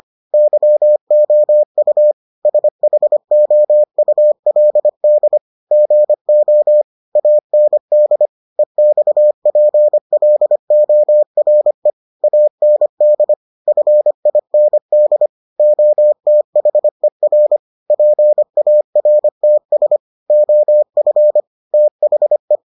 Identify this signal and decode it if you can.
Morse code, heard as a keyed tone, about 25 wpm, 600 Hz. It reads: YOU SHOULD GO AND EXPLORE AND FIND OTHER PARTS OF THE